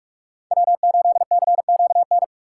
Morse code, heard as a keyed tone, about 43 wpm, 700 Hz. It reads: W8CQN
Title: USA Call Signs - 5 Characters - 3x Character Spacing - 45wpm